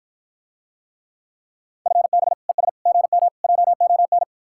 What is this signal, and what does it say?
Morse code, heard as a keyed tone, about 54 wpm, 700 Hz. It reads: FB ES GM JON